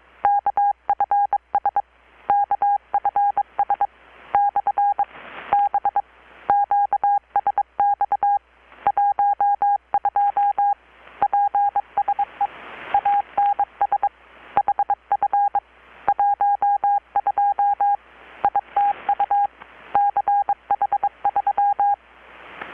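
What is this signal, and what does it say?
Morse code, heard as a keyed tone, about 23 wpm, spaced as for 21 wpm, 810 Hz. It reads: KFS KFS/B QSX 12 PSE ANS HF 12 ITU CH3
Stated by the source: Morse from KPH's Night of Nights XX maritime radiotelegraphy event, received in Chiapas, Mexico about 3,800 km away on a Kenwood R-600 with a grounded 10 m long-wire antenna.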